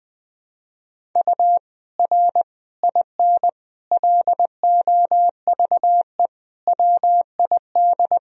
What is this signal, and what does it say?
Morse code, heard as a keyed tone, about 20 wpm, 700 Hz. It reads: U R IN LOVE WID
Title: Sentences from Top 400 Words and Abbreviations 20wpm